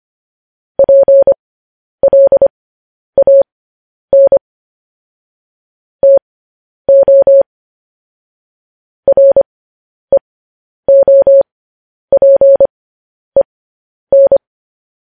Morse code, heard as a keyed tone, about 25 wpm, spaced as for 10 wpm, 550 Hz. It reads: PLAN TO REOPEN
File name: podcast / News headlines-1x-25x10